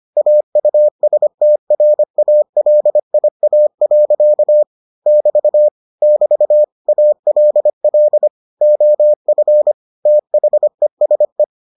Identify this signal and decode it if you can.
Morse code, heard as a keyed tone, about 25 wpm, 600 Hz. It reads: AUSTRALIA. <BT> <BT> ALL OF THESE